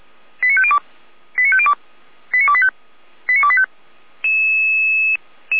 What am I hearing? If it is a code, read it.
Five-tone selective call (ZVEI), heard as quick runs of five tones, tones of about 70 ms. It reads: 89502 89502 89285 89285